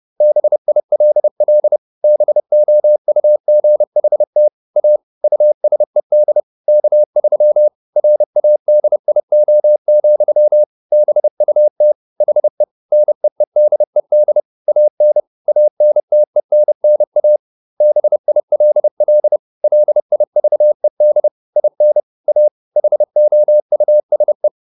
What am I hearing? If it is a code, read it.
Morse code, heard as a keyed tone, about 30 wpm, 600 Hz. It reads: BILL BOUGHT A USED K3 RADIO, BUT HE NEEDED AN ANTENNA BILL LIVED IN A HOUSE
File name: SS302 30